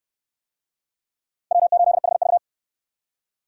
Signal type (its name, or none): Morse code